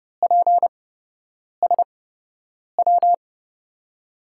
Morse code, heard as a keyed tone, about 30 wpm, spaced as for 8 wpm, 700 Hz. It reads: PSW
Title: USA Call Signs - 6 Characters - 8x Character Spacing - 30wpm